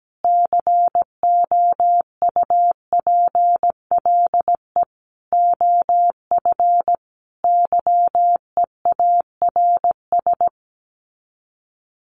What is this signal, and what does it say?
Morse code, heard as a keyed tone, about 17 wpm, 700 Hz. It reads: COUPLE OF YEARS
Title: Sets of 5 Words Encore 17wpm